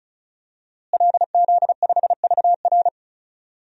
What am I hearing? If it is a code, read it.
Morse code, heard as a keyed tone, about 34 wpm, 700 Hz. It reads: LZ5VR
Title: DXCC-100 Call Signs - 5 Characters - Repeated 2x - 35wpm